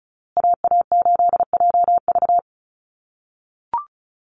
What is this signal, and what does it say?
Morse code, heard as a keyed tone, about 34 wpm, 700 Hz. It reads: AA8JV
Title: USA Call Signs Repeated 2x 35wpm